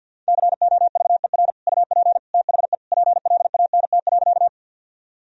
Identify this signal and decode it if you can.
Morse code, heard as a keyed tone, about 52 wpm, 700 Hz. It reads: COVER UP THE PLANT.
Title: Sentences from Top 1000 Words - Review Part 1 - 50wpm